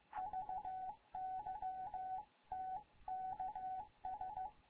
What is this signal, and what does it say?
Morse code, heard as a keyed tone, about 15 wpm, 750 Hz. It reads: VY TKS
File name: Morze-HA2MN / hangp / 2 11 GYAKORLAT